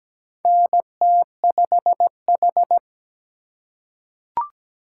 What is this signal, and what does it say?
Morse code, heard as a keyed tone, about 17 wpm, 700 Hz. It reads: NT5H
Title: USA Call Signs - 4 Characters - 4x Character Spacing - 17wpm